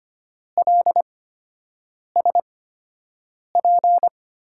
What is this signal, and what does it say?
Morse code, heard as a keyed tone, about 25 wpm, spaced as for 7 wpm, 700 Hz. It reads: LSP
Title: USA Call Signs - 6 Characters - 8x Character Spacing - 25wpm